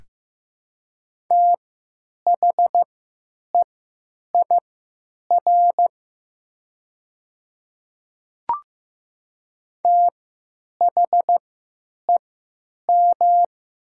Morse code, heard as a keyed tone, about 15 wpm, spaced as for 8 wpm, 700 Hz. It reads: THEIR THEM
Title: 3x Character Spacing - Top 100 Words - 15wpm